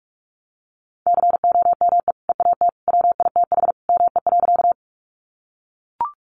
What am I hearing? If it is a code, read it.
Morse code, heard as a keyed tone, about 44 wpm, 700 Hz. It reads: COME EAT WITH ME.
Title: Sentences from Top 300 Words 45wpm